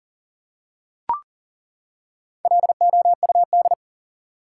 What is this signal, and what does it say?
Morse code, heard as a keyed tone, about 40 wpm, 700 Hz. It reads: LOUD
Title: Four Letter Words - 40wpm